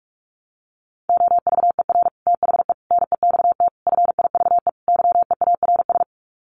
Morse code, heard as a keyed tone, about 44 wpm, 700 Hz. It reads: OVER THE NEXT FIVE YEARS
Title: Sets of 5 Words 45wpm